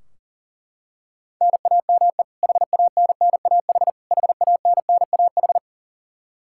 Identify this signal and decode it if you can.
Morse code, heard as a keyed tone, about 40 wpm, 700 Hz. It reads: NAME HANNAH HANNAH
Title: Practice QSO - Exchange 1 - Beginning - 40wpm